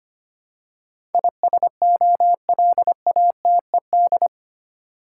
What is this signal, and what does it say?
Morse code, heard as a keyed tone, about 25 wpm, 700 Hz. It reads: ISOLATED
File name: 8 Letter Words - First 3 Letters Given - 25wpm